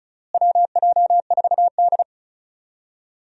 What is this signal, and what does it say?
Morse code, heard as a keyed tone, about 34 wpm, 700 Hz. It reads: WJ4D
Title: USA Call Signs - 4 Characters - Repeated 3x - 35wpm